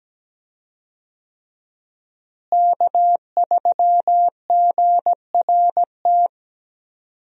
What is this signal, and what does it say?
Morse code, heard as a keyed tone, about 17 wpm, 700 Hz. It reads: K3GRT
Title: USA Call Signs - 5 Characters - Repeated 3x - 17wpm